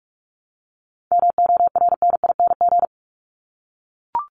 Morse code, heard as a keyed tone, about 44 wpm, 700 Hz. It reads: MORNING
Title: Top 500 Words Repeated 45wpm